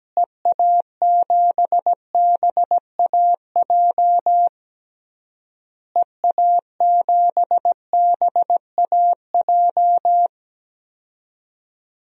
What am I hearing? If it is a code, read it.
Morse code, heard as a keyed tone, about 17 wpm, 700 Hz. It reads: EA7BAJ EA7BAJ
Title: DXCC-100 Call Signs - 6 Characters - Repeated 3x - 17wpm